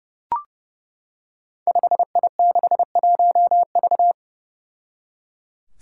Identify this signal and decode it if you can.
Morse code, heard as a keyed tone, about 30 wpm, 700 Hz. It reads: 5I61V
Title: Five Challenging Letter-Number Combinations 30wpm